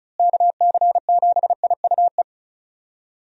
Morse code, heard as a keyed tone, about 36 wpm, 700 Hz. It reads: KC7IUE
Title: USA Call Signs - 6 Characters - Repeated 3x - 35wpm